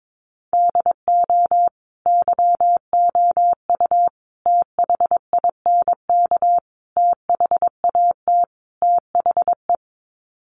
Morse code, heard as a keyed tone, about 22 wpm, 700 Hz. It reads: DO YOU THINK THAT THE
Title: Sets of 5 Words Encore 22wpm